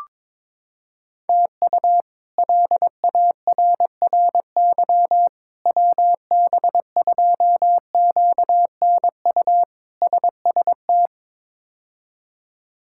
Morse code, heard as a keyed tone, about 22 wpm, 700 Hz. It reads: TU LARRY WB2QNU SST
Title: K1USN SST QSO Practice - 22wpm